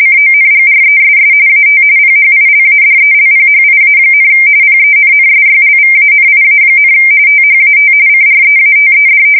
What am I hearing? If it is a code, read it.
RTTY, about 45.45 baud. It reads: CONGRESS AT PHILADELPHIA, PENNSYLVANIA, THAT CREATED
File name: fsk170 / usnavy1